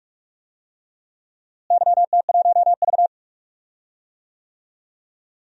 Morse code, heard as a keyed tone, about 46 wpm, 700 Hz. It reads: YT1V
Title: DXCC-100 Call Signs - 4 Characters - Repeated 2x - 45wpm